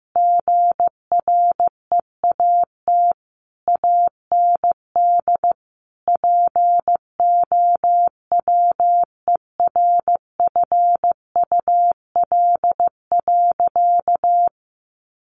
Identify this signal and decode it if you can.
Morse code, heard as a keyed tone, about 15 wpm, 700 Hz. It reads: GREAT AND POWERFUL.
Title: Sentences from Top 1300 Words 15wpm